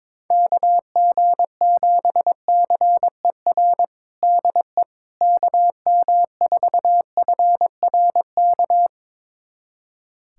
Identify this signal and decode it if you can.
Morse code, heard as a keyed tone, about 21 wpm, 700 Hz. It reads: KG7CER DE KM4FRK
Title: USA Callsign DE Callsign 22wpm